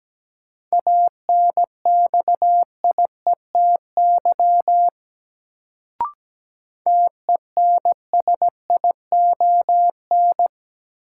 Morse code, heard as a keyed tone, about 17 wpm, 700 Hz. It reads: ANXIETY TENSION
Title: Sentences from Top 1600 Words - New Word Drill - 17wpm